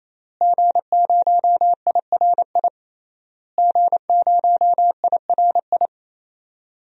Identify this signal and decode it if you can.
Morse code, heard as a keyed tone, about 27 wpm, 700 Hz. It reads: G0IRI G0IRI
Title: DXCC-100 Call Signs - 5 Characters - Repeated 3x - 28wpm